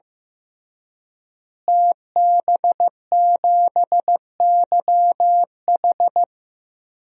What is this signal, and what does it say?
Morse code, heard as a keyed tone, about 15 wpm, 700 Hz. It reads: TB7YH
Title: DXCC-100 Call Signs - 5 Characters - Repeated 3x - 15wpm